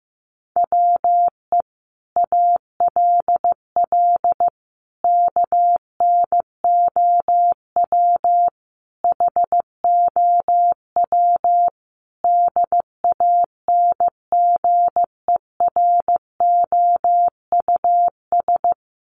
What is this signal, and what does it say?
Morse code, heard as a keyed tone, about 15 wpm, 700 Hz. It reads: WE ALL KNOW HOW DANGEROUS
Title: Sentences from Top 1200 Words 15wpm